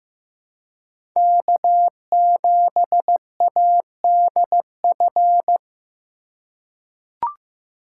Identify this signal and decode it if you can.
Morse code, heard as a keyed tone, about 15 wpm, 700 Hz. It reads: K7ADF
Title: USA Call Signs - 5 Characters - 4x Character Spacing - 15wpm